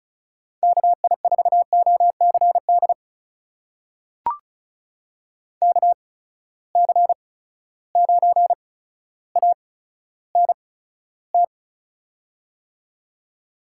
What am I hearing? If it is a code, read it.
Morse code, heard as a keyed tone, about 36 wpm, spaced as for 15 wpm, 700 Hz. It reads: KI4OCD KC9ANT
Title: USA Call Signs - 6 Characters - 8x Character Spacing - 35wpm